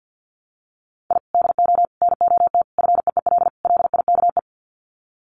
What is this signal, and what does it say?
Morse code, heard as a keyed tone, about 50 wpm, 700 Hz. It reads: I DO NOT FEEL LIKE
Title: Sets of 5 Words Encore 50wpm